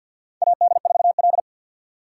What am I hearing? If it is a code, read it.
Morse code, heard as a keyed tone, about 53 wpm, 700 Hz. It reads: AD4L
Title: USA Call Signs - 4 Characters - 3x Character Spacing - 50wpm